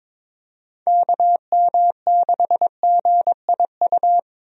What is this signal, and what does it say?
Morse code, heard as a keyed tone, about 22 wpm, 700 Hz. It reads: KM6GIU
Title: USA Call Signs - 6 Characters - Repeated 2x - 22wpm